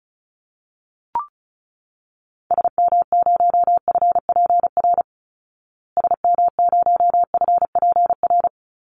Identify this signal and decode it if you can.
Morse code, heard as a keyed tone, about 34 wpm, 700 Hz. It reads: SM0FPR SM0FPR
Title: Call Signs Repeated 3x - 35wpm